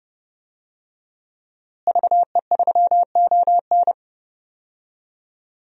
Morse code, heard as a keyed tone, about 30 wpm, 700 Hz. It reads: VE3ON
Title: DXCC-100 Call Signs - 5 Characters - Repeated 2x - 30wpm